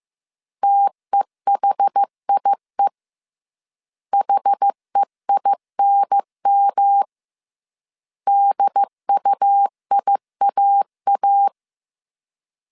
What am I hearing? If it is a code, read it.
Morse code, heard as a keyed tone, about 15 wpm, 800 Hz. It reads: TEHIE HEINM DUIAA